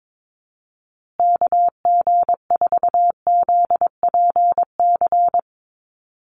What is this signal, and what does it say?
Morse code, heard as a keyed tone, about 22 wpm, 700 Hz. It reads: KG4ZPC
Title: USA Call Signs Repeated 2x 22wpm